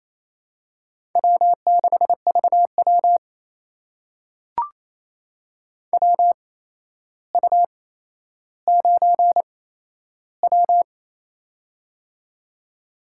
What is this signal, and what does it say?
Morse code, heard as a keyed tone, about 27 wpm, spaced as for 12 wpm, 700 Hz. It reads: W6VW WU9W
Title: USA Call Signs - 4 Characters - 8x Character Spacing - 28wpm